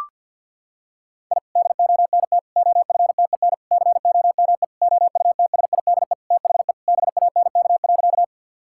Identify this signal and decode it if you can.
Morse code, heard as a keyed tone, about 51 wpm, 700 Hz. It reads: I DONT OFTEN COME OUTSIDE THE BANK.
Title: Sentences from Top 1000 Words - Review Part 6 - 50wpm